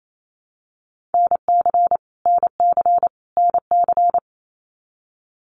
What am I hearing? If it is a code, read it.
Morse code, heard as a keyed tone, about 28 wpm, 700 Hz. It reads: NC NC NC
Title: US States Repeated 3x 28wpm